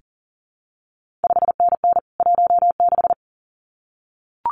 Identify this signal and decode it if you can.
Morse code, heard as a keyed tone, about 39 wpm, 700 Hz. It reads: 5NN 16
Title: CQ WW DX Contest Exchange Encore - 40wpm